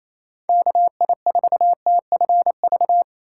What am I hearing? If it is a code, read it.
Morse code, heard as a keyed tone, about 28 wpm, 700 Hz. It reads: KI4TFV
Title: USA Call Signs - 6 Characters - 8x Character Spacing - 28wpm